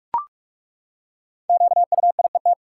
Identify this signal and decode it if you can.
Morse code, heard as a keyed tone, about 42 wpm, 700 Hz. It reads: QUIET
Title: Five Letter Words - 45wpm